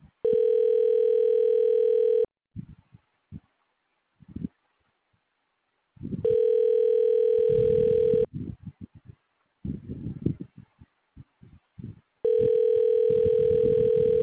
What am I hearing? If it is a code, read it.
Ringback tone (North America), 440 + 480 Hz.
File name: recordings / RE d0517caf37bb1e20b224f65e7